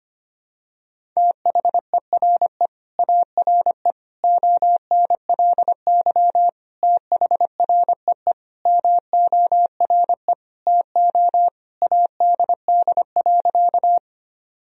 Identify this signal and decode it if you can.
Morse code, heard as a keyed tone, about 25 wpm, 700 Hz. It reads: THERE ARE ONLY THREE MORE TO ADD.
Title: Sentences from Top 1000 Words - Review Part 1 - 25wpm